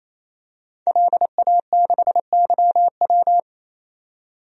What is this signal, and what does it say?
Morse code, heard as a keyed tone, about 27 wpm, 700 Hz. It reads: LA6YW
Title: DXCC-100 Call Signs - 5 Characters - 28wpm